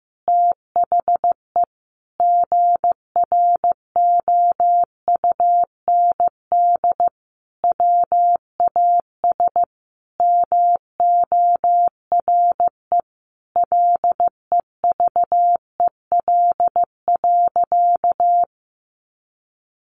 Morse code, heard as a keyed tone, about 15 wpm, 700 Hz. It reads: THE GROUND WAS MORE LEVEL.
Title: Sentences from Top 700 Words 15wpm